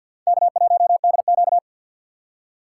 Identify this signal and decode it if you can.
Morse code, heard as a keyed tone, about 51 wpm, 700 Hz. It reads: K1DQ
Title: USA Call Signs - 2x Character Spacing - 50wpm